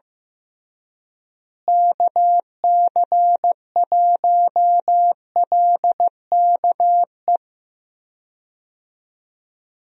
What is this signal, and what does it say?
Morse code, heard as a keyed tone, about 15 wpm, 700 Hz. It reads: KC1LKE
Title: USA Call Signs - 6 Characters - Repeated 3x - 15wpm